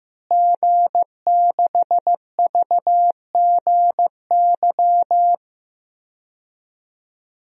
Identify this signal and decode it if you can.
Morse code, heard as a keyed tone, about 15 wpm, 700 Hz. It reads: G6VGY
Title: DXCC-100 Call Signs - 5 Characters - Repeated 2x - 15wpm